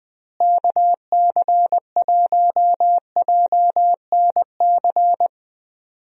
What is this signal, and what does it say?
Morse code, heard as a keyed tone, about 20 wpm, 700 Hz. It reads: KC1JNC
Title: USA Call Signs - 6 Characters - Repeated 2x - 20wpm